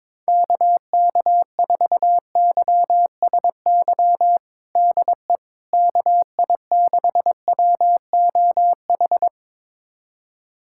Morse code, heard as a keyed tone, about 22 wpm, 700 Hz. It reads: KK4YSY DE KI6WOH
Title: USA Callsign DE Callsign 22wpm